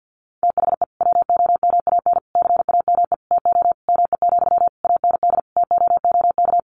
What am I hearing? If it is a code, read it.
Morse code, heard as a keyed tone, about 48 wpm, 700 Hz. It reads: THE WOMAN CAME TO ME, AND TOOK
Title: Sentences from Top 1100 Words 50wpm